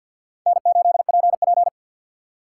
Morse code, heard as a keyed tone, about 48 wpm, 700 Hz. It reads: N8PP
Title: USA Call Signs - 4 Characters - 6x Character Spacing - 50wpm